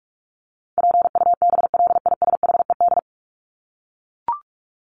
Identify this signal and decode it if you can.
Morse code, heard as a keyed tone, about 44 wpm, 700 Hz. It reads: PUBLISHED
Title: ED Suffix - Top 100 Words with Base Words 45wpm